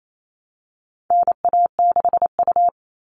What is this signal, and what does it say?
Morse code, heard as a keyed tone, about 28 wpm, 700 Hz. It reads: NA6U
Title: All Cut Numbers - 4-Digit 28wpm